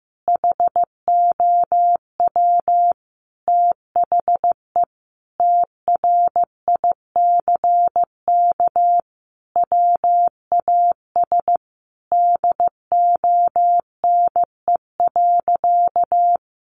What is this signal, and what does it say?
Morse code, heard as a keyed tone, about 15 wpm, 700 Hz. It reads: HOW THE TRICK WAS DONE.
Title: Sentences from Top 1500 Words 15wpm